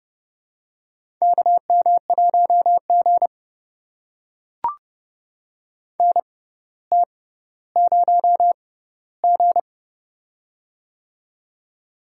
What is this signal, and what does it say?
Morse code, heard as a keyed tone, about 31 wpm, spaced as for 16 wpm, 700 Hz. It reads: KM1G NT0G
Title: USA Call Signs - 4 Characters - 6x Character Spacing - 30wpm